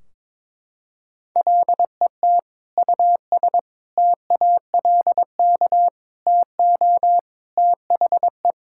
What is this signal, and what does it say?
Morse code, heard as a keyed tone, about 22 wpm, 700 Hz. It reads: LET US TALK TO THE
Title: Sentences from Top 1600 Words 22wpm